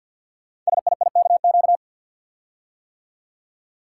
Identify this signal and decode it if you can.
Morse code, heard as a keyed tone, about 50 wpm, 690 Hz. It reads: SIIKQ
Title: Code Groups - 5 Letters - 50wpm